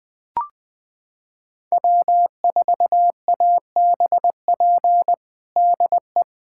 Morse code, heard as a keyed tone, about 20 wpm, 700 Hz. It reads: W4ABP DE
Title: USA Callsign DE Callsign 20wpm